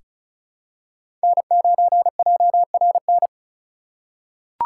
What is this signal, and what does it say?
Morse code, heard as a keyed tone, about 36 wpm, 700 Hz. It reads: N9JRN
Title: USA Call Signs - 5 Characters - 3x Character Spacing - 35wpm